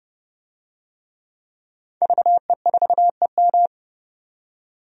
Morse code, heard as a keyed tone, about 30 wpm, 700 Hz. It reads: VE4EM
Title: DXCC-100 Call Signs - 5 Characters - 30wpm